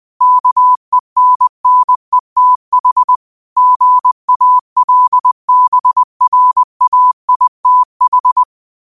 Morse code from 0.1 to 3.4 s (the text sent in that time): KENNETH